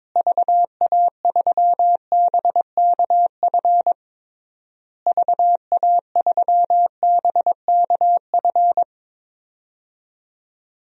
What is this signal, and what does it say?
Morse code, heard as a keyed tone, about 22 wpm, 700 Hz. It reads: VA3BKF VA3BKF
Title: DXCC-100 Call Signs - 6 Characters - Repeated 3x - 22wpm